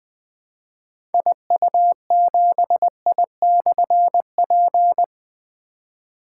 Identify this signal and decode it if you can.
Morse code, heard as a keyed tone, about 20 wpm, 700 Hz. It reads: IU7I/P
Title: Difficult Call Signs 20wpm